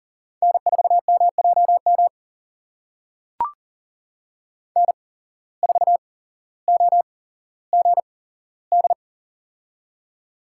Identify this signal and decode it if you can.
Morse code, heard as a keyed tone, about 40 wpm, spaced as for 17 wpm, 700 Hz. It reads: N4MJM N4OGD
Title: USA Call Signs - 5 Characters - 8x Character Spacing - 40wpm